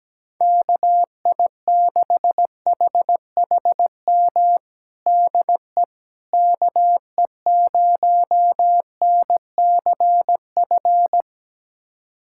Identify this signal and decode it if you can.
Morse code, heard as a keyed tone, about 17 wpm, 700 Hz. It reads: KI6HHM DE KE0NCF
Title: USA Callsign DE Callsign 17wpm